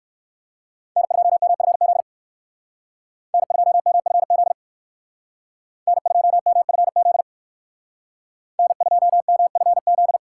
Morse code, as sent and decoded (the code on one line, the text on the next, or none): -. ..--- -- ..-. --.. / -. ..--- -- ..-. --.. / -. ..--- -- ..-. --.. / -. ..--- -- ..-. --..
N2MFZ N2MFZ N2MFZ N2MFZ